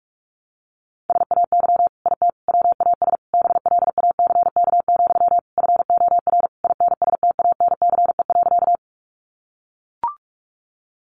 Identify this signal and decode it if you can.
Morse code, heard as a keyed tone, about 44 wpm, 700 Hz. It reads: SAY IT WAS BLACK, FOR INSTANCE.
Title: Sentences from Top 800 Words 45wpm